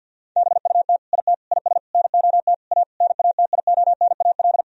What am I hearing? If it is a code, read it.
Morse code, heard as a keyed tone, about 50 wpm, 700 Hz. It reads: BUT IT IS NOT A NATIONAL
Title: Sentences from Top 1000 Words - Review Part 4 - 50wpm